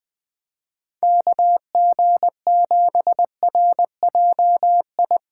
Morse code, heard as a keyed tone, about 20 wpm, 700 Hz. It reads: KG7RJI
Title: Speed Racing - USA 2x3 Call Signs 17wpm